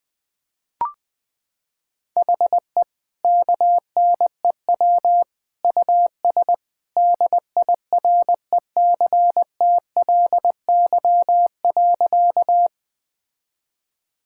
Morse code, heard as a keyed tone, about 20 wpm, 700 Hz. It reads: HE KNEW US DIRECTLY.
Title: Sentences from Top 1000 Words - Review Part 5 - 20wpm